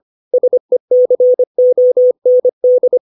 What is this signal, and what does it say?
Morse code, heard as a keyed tone, about 25 wpm, 500 Hz. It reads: SECOND
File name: 632 words / 632 Words 25WPM